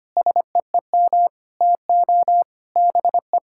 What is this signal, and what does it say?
Morse code, heard as a keyed tone, about 25 wpm, 700 Hz. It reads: SEEM TO BE
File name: practice / 25wpm / Sentences with 3-Word Phrases - 25wpm